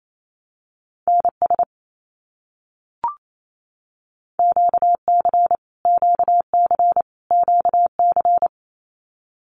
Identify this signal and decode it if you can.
Morse code, heard as a keyed tone, about 28 wpm, 700 Hz. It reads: NS QC QC QC
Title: Canadian Provinces and Territories - Repeated 3x - 28wpm